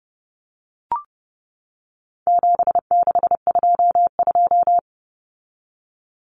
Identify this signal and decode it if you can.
Morse code, heard as a keyed tone, about 30 wpm, 700 Hz. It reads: 7622
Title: Common Cut Numbers - 4-Digit 30wpm